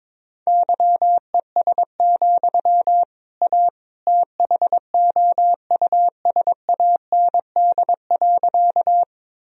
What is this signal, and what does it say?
Morse code, heard as a keyed tone, about 22 wpm, 700 Hz. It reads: YES, A THOUSAND.
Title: Sentences from Top 1000 Words - Review Part 3 - 22wpm